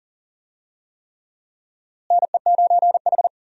Morse code, heard as a keyed tone, about 40 wpm, 700 Hz. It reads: NE9H
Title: USA Call Signs - 4 Characters - Repeated 3x - 40wpm